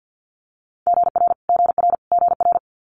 Morse code, heard as a keyed tone, about 50 wpm, 700 Hz. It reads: GR GR GR
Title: Multi-letter Phonemes Repeated 3x 50wpm